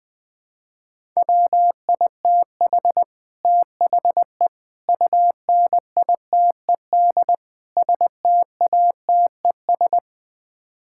Morse code, heard as a keyed tone, about 21 wpm, 700 Hz. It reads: WITH THE UNITED STATES
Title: Build A Phrase - 4-Word Phrases - 20wpm